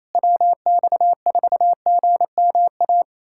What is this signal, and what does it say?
Morse code, heard as a keyed tone, about 28 wpm, 700 Hz. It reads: WX4GMA
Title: USA Callsign DE Callsign - 6 Characters - 28wpm